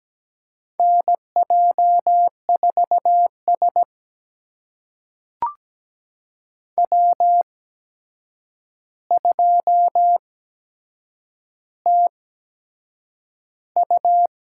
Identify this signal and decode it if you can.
Morse code, heard as a keyed tone, about 17 wpm, spaced as for 7 wpm, 700 Hz. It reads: NJ4S W2TU
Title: USA Call Signs - 8x Character Spacing - 17wpm